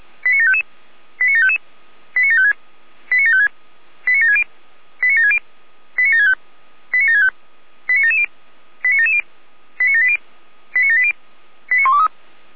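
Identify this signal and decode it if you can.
Five-tone selective call (ZVEI), heard as quick runs of five tones, tones of about 70 ms. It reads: 89755 89755 89756 89756 89760 89760 89765 89765 89770 89770 89780 89780 89123
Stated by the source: Received on an SDR, ZVEI five-tone selective call.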